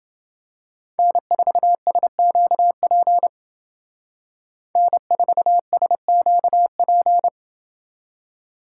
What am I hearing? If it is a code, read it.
Morse code, heard as a keyed tone, about 28 wpm, 700 Hz. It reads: N4SQP N4SQP